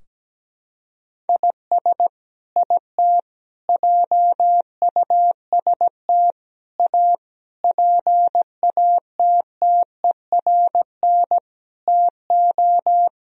Morse code, heard as a keyed tone, about 17 wpm, 700 Hz. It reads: IS IT JUST A PATTERN TO
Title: Sentences from Top 1000 Words - Review Part 2 - 17wpm